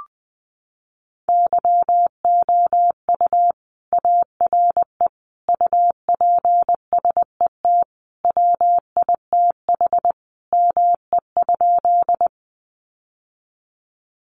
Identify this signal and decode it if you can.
Morse code, heard as a keyed tone, about 20 wpm, 700 Hz. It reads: YOU ARE UPSET WITH ME?
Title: Sentences from Top 800 Words 20wpm